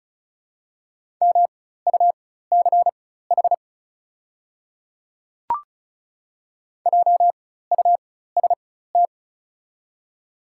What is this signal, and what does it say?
Morse code, heard as a keyed tone, about 34 wpm, spaced as for 16 wpm, 700 Hz. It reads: MUCH JUST
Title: 4x Character Spacing - Top 200 Words - 35wpm